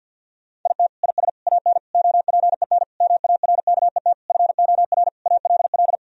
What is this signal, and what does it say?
Morse code, heard as a keyed tone, about 50 wpm, 700 Hz. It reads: IT IS AN OPEN MARKET FOR ALL
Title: Sentences from Top 1000 Words - Review Part 3 - 50wpm